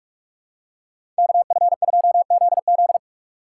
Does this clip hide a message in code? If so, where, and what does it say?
Yes: Morse code from 1.2 to 3.0 s, heard as a keyed tone, about 42 wpm, 700 Hz. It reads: KF2ZZ